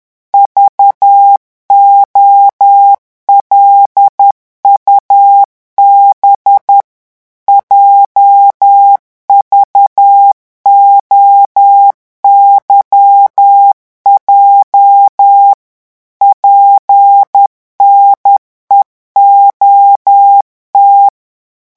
Morse code, heard as a keyed tone, about 10 wpm, 800 Hz. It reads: VOLUB JVOYJ PNEOT
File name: morse mp3 / revision8-vitesse 10 mots